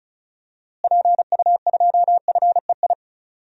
Morse code, heard as a keyed tone, about 36 wpm, 700 Hz. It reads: PU2FEI
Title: DXCC-100 Call Signs - 6 Characters - Repeated 3x - 35wpm